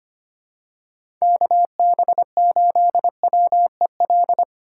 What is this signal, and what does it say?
Morse code, heard as a keyed tone, about 25 wpm, 700 Hz. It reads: KB8WEL
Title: USA Call Signs - 6 Characters - 2x Character Spacing - 25wpm